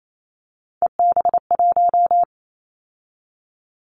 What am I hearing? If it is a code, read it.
Morse code, heard as a keyed tone, about 28 wpm, 700 Hz. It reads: EB1
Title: Callsign Prefixes for Top 25 DXCC - 28wpm